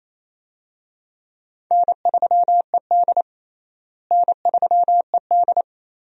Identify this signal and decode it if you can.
Morse code, heard as a keyed tone, about 28 wpm, 700 Hz. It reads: N3ED N3ED
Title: USA Call Signs - 4 Characters - Repeated 3x - 28wpm